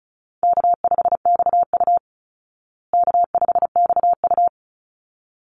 Morse code, heard as a keyed tone, about 34 wpm, 700 Hz. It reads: K5XU K5XU
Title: Call Signs Repeated 3x - 35wpm